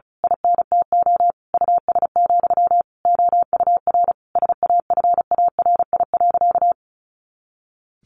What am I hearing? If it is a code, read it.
Morse code, heard as a keyed tone, about 34 wpm, 700 Hz. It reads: INTO US, OUR SAFARI.
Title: A Sound Of Thunder 35wpm